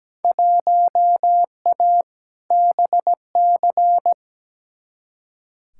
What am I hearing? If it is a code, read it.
Morse code, heard as a keyed tone, about 17 wpm, 700 Hz. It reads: 1A BC
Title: Field Day Exchanges - 17wpm